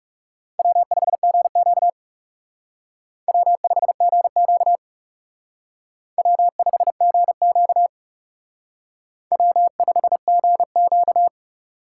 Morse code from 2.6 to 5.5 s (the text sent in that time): W5GQ